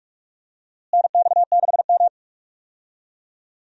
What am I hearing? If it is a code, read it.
Morse code, heard as a keyed tone, about 42 wpm, 700 Hz. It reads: NX6M
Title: USA Call Signs - 4 Characters - Repeated 2x - 45wpm